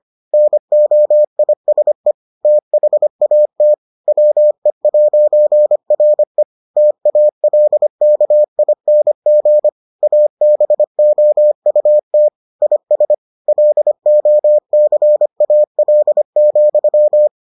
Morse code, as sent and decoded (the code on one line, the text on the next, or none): -. --- .. ... / - .... .- - / .-- . .----. .-. / - .- .-.. -.- .. -. --. / .- -... --- ..- - / .. ... / .-.. --- -.-. .- .-.. --..--
NOISE THAT WE'RE TALKING ABOUT IS LOCAL,